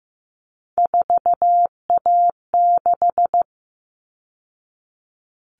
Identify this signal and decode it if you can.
Morse code, heard as a keyed tone, about 15 wpm, 700 Hz. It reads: 4A6
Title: All Cut Numbers - 3-Digit 15wpm